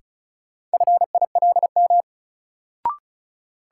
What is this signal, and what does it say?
Morse code, heard as a keyed tone, about 36 wpm, 700 Hz. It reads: FILM
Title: Sentences from Top 1000 Words - Review Part 3 - New Word Drill - 35wpm